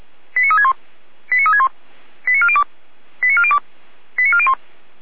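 Five-tone selective call (ZVEI), heard as quick runs of five tones, tones of about 70 ms. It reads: 89361 89361 89402 89402 89401
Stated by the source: ZVEI five-tone selective call received by SDR.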